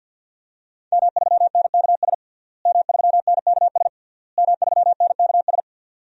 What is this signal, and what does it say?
Morse code, heard as a keyed tone, about 52 wpm, 700 Hz. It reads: M3NKS M3NKS M3NKS
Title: DXCC-100 Call Signs - 5 Characters - Repeated 3x - 50wpm